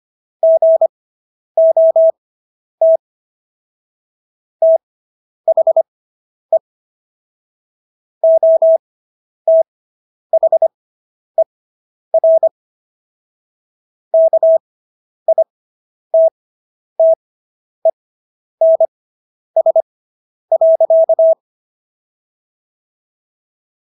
Morse code, heard as a keyed tone, about 25 wpm, spaced as for 10 wpm, 650 Hz. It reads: GOT THE OTHER KITTENS.